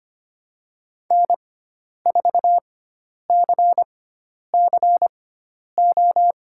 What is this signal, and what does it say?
Morse code, heard as a keyed tone, about 25 wpm, spaced as for 10 wpm, 700 Hz. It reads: N4CCO
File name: Lesson 16g - Letter c - Call Signs 25wpm